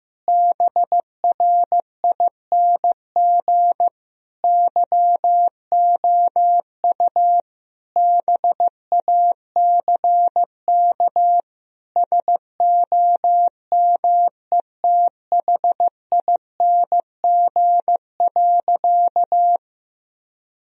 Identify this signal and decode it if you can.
Morse code, heard as a keyed tone, about 15 wpm, 700 Hz. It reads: BRING YOU BACK SOMETHING.
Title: Sentences from Top 1000 Words - Review Part 3 - 15wpm